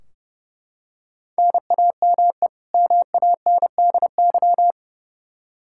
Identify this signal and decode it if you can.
Morse code, heard as a keyed tone, about 30 wpm, 700 Hz. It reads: NAME MANDY
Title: Practice QSO - Exchange 1 - Beginning - 30wpm